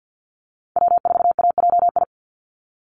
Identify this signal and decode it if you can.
Morse code, heard as a keyed tone, about 50 wpm, 700 Hz. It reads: W4AJI
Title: USA Call Signs 50wpm